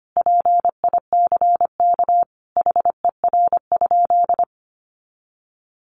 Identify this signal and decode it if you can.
Morse code, heard as a keyed tone, about 25 wpm, 700 Hz. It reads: PICK HER?
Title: Sentences from Top 1000 Words 25wpm